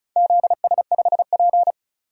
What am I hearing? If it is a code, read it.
Morse code, heard as a keyed tone, about 34 wpm, 700 Hz. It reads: ZS5P
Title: DXCC-100 Call Signs - 4 Characters - 35wpm